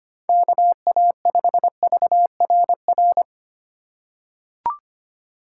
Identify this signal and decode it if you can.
Morse code, heard as a keyed tone, about 25 wpm, 700 Hz. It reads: KA5VRR
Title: USA Call Signs - 6 Characters - 6x Character Spacing - 25wpm